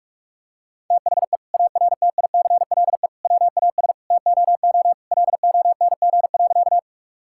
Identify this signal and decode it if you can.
Morse code, heard as a keyed tone, about 41 wpm, 700 Hz. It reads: THE ARTICLE WAS TOO LONG.
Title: Sentences from Top 1000 Words - Review Part 5 - 45wpm